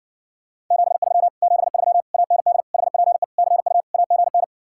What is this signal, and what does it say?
Morse code, heard as a keyed tone, about 69 wpm, 700 Hz. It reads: 73 73 AND HPE CU AGN